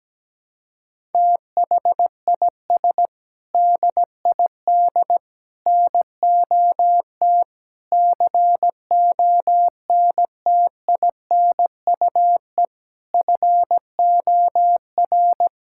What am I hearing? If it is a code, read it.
Morse code, heard as a keyed tone, about 17 wpm, 700 Hz. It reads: THIS DID NOT CONTINUE FOR
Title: Sentences from Top 1000 Words - Review Part 6 - 17wpm